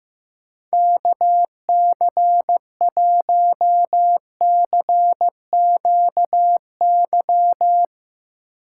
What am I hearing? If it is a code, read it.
Morse code, heard as a keyed tone, about 15 wpm, 700 Hz. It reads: KC1CQY
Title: USA Call Signs - 6 Characters - 8x Character Spacing - 15wpm